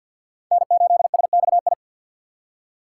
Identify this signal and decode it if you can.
Morse code, heard as a keyed tone, about 62 wpm, 700 Hz. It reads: N8SXI
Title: Speed Racing - USA Call Signs 50wpm